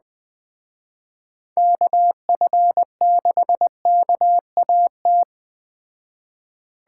Morse code, heard as a keyed tone, about 21 wpm, 700 Hz. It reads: KF6KAT